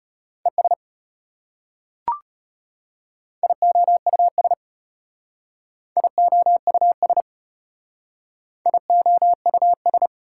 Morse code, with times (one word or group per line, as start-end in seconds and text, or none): 0.5-0.7 ES
3.4-4.5 IOUS
6.0-7.2 IOUS
8.7-10.1 IOUS